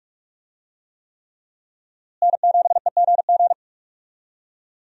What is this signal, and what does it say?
Morse code, heard as a keyed tone, about 47 wpm, 700 Hz. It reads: N7EGG
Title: USA Call Signs - 5 Characters - Repeated 3x - 45wpm